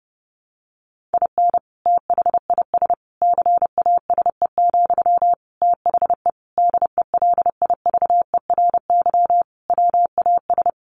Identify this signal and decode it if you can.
Morse code, heard as a keyed tone, about 30 wpm, 700 Hz. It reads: IN THIS CASE, THE DELIVERY WAS
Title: Sentences from Top 1500 Words 30wpm